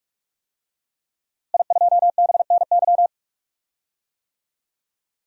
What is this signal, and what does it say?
Morse code, heard as a keyed tone, about 46 wpm, 700 Hz. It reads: I2BNY
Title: DXCC-100 Call Signs - 5 Characters - Repeated 3x - 45wpm